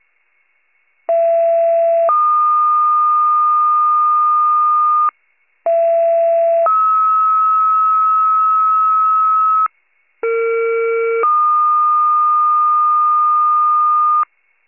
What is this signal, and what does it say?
Two-tone page, heard as a one-second tone, then a three-second tone, about 670 then 1220 Hz, about 670 then 1290 Hz, about 460 then 1220 Hz. It is heard from 1.1 to 14.2 s.